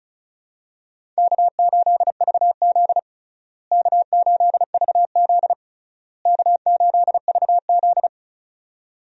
Morse code, heard as a keyed tone, about 36 wpm, 700 Hz. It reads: K8VZ K8VZ K8VZ
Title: USA Call Signs - 4 Characters - Repeated 3x - 35wpm